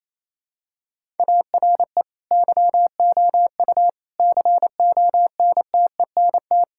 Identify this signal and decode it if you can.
Morse code, heard as a keyed tone, about 27 wpm, 700 Hz. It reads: ARE YOU CONTENT
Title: Sentences from Top 1000 Words - Review Part 6 - 28wpm